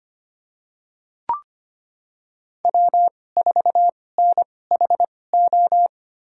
Morse code, heard as a keyed tone, about 25 wpm, spaced as for 18 wpm, 700 Hz. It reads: W4NHO
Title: USA Call Signs - 5 Characters - 2x Character Spacing - 25wpm